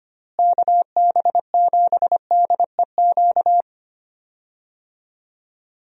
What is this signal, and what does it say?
Morse code, heard as a keyed tone, about 25 wpm, 700 Hz. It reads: KB7DEQ
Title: USA Call Signs - 6 Characters - Repeated 2x - 25wpm